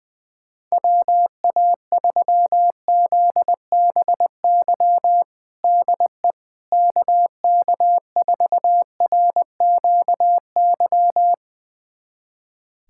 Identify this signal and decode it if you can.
Morse code, heard as a keyed tone, about 20 wpm, 700 Hz. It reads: WA3ZBY DE KK4RQY
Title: USA Callsign DE Callsign 20wpm